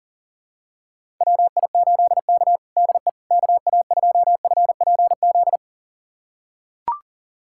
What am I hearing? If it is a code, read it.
Morse code, heard as a keyed tone, about 40 wpm, 700 Hz. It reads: WI8K DE KA2FPZ
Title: USA Callsign DE Callsign 40wpm